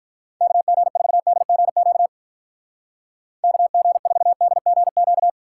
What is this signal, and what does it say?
Morse code, heard as a keyed tone, about 44 wpm, 700 Hz. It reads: KG4DGQ KG4DGQ
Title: Speed Racing - USA Call Signs 35wpm